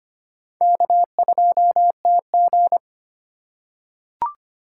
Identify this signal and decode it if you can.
Morse code, heard as a keyed tone, about 25 wpm, 700 Hz. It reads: K2TG
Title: USA Call Signs - 4 Characters - 4x Character Spacing - 25wpm